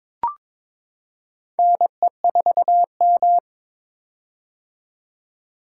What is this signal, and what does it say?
Morse code, heard as a keyed tone, about 22 wpm, 700 Hz. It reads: NE4M
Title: K1USN SST QSO Practice - 22wpm